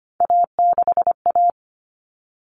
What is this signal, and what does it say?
Morse code, heard as a keyed tone, about 25 wpm, 700 Hz. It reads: A6A
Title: Common Cut Numbers - 3-Digit 25wpm